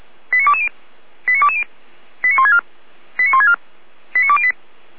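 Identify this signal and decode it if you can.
Five-tone selective call (ZVEI), heard as quick runs of five tones, tones of about 80 ms. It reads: 89220 89220 89164 89164 89208
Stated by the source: ZVEI five-tone call 89150 received on an SDR.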